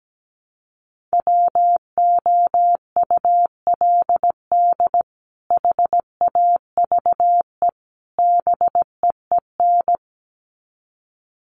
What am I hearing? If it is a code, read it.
Morse code, heard as a keyed tone, about 17 wpm, 700 Hz. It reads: WOULD HAVE BEEN
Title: Sets of 3 Words 17wpm